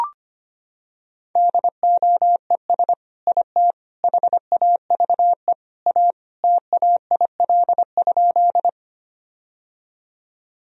Morse code, heard as a keyed tone, about 25 wpm, 700 Hz. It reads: DOES IT HAVE A TAIL?
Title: Sentences from Top 1000 Words - Review Part 3 - 25wpm